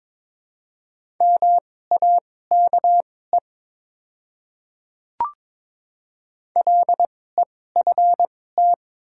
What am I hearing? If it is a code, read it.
Morse code, heard as a keyed tone, about 23 wpm, spaced as for 16 wpm, 700 Hz. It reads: MAKE LEFT
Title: 2x Character Spacing - Top 300 Words - 22wpm